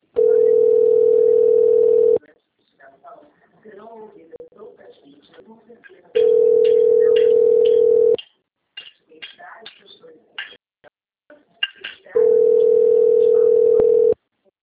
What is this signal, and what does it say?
Ringback tone (North America), 440 + 480 Hz, with speech.
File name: rg-1600-unknown- - - .37628